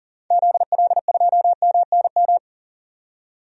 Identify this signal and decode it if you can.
Morse code, heard as a keyed tone, about 40 wpm, 700 Hz. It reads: ZL2MNM
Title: DXCC-100 Call Signs - 6 Characters - 40wpm